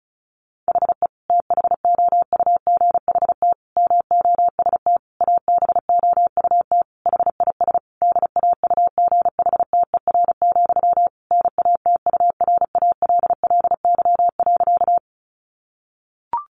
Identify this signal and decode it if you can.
Morse code, heard as a keyed tone, about 34 wpm, 700 Hz. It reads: HE THOUGHT MOST ABOUT HIS DAUGHTER, NATURALLY.
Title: Sentences from Top 1500 Words 35wpm